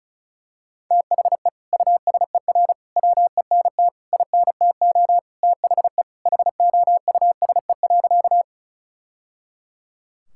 Morse code, heard as a keyed tone, about 36 wpm, 700 Hz. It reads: THE USER WENT INTO THE HOUSE.
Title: Sentences from Top 1000 Words - Review Part 5 - 35wpm